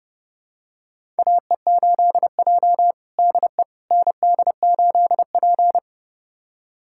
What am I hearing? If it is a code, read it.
Morse code, heard as a keyed tone, about 29 wpm, 700 Hz. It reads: AE8J DE ND8P
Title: USA Callsign DE Callsign - 4 Characters - 30wpm